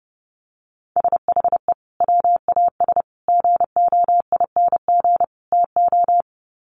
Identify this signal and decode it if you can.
Morse code, heard as a keyed tone, about 30 wpm, 700 Hz. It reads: SHE WAS GOING TO
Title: Sets of 4 Words 30wpm